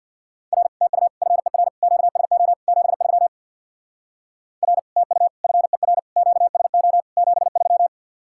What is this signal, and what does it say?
Morse code, heard as a keyed tone, about 67 wpm, 700 Hz. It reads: R TU FER QSO 73 R TU FER QSO 73